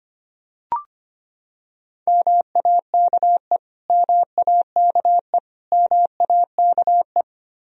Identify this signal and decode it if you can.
Morse code, heard as a keyed tone, about 25 wpm, 700 Hz. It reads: MAKE MAKE MAKE
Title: Top 100 Words Repeated 25wpm